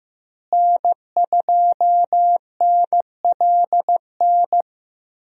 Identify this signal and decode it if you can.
Morse code, heard as a keyed tone, about 15 wpm, 700 Hz. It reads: N2NLN
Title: USA Call Signs - 5 Characters - Repeated 2x - 15wpm